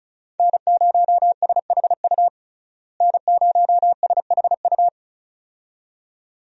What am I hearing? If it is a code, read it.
Morse code, heard as a keyed tone, about 34 wpm, 700 Hz. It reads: N0SHU N0SHU
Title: USA Call Signs - 5 Characters - Repeated 2x - 35wpm